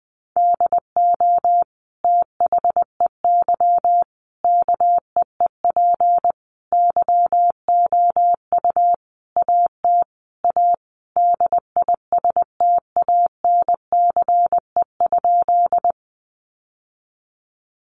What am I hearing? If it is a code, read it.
Morse code, heard as a keyed tone, about 20 wpm, 700 Hz. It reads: DO THEY KEEP YOU AT A DISTANCE?
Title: Sentences from Top 1200 Words 20wpm